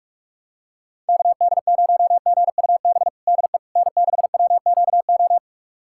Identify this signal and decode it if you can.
Morse code, heard as a keyed tone, about 46 wpm, 700 Hz. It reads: KD0GUB DE N6WQO
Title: USA Callsign DE Callsign 45wpm